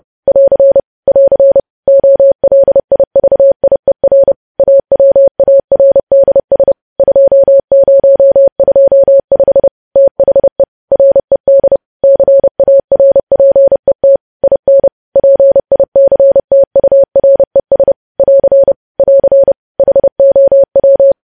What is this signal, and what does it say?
Morse code, heard as a keyed tone, about 30 wpm, 550 Hz. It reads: <AR> <AR> OLIVIER AWARDS 2025 THE RED CARPET IN PICTURES <AR> <AR> HOW
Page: news headlines in morse code at 30 wpm